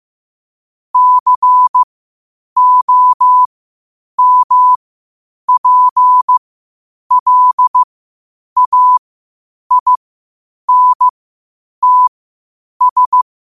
Morse code, heard as a keyed tone, about 15 wpm, spaced as for 8 wpm, 1000 Hz. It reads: COMPLAINTS
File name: qotdfiles / 8wpm